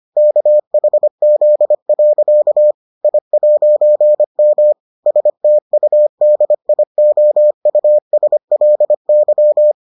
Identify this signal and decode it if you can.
Morse code, heard as a keyed tone, about 25 wpm, 600 Hz. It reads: KHZ. I'M STUDIOUSLY